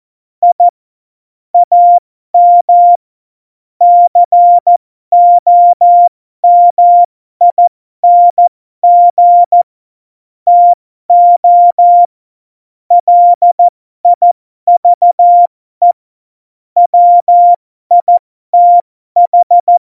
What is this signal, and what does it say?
Morse code, heard as a keyed tone, about 14 wpm, spaced as for 12 wpm, 700 Hz. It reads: I AM COMING TO LIVE WITH
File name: Morsekurs / Witze